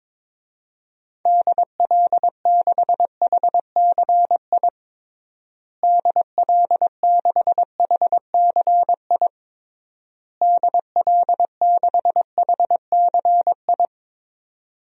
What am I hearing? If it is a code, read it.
Morse code, heard as a keyed tone, about 22 wpm, 700 Hz. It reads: DL6HCI DL6HCI DL6HCI
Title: DXCC-100 Call Signs - 6 Characters - Repeated 3x - 22wpm